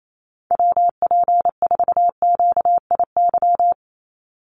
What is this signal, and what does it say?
Morse code, heard as a keyed tone, about 28 wpm, 700 Hz. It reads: WP4QIY
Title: USA Call Signs 28wpm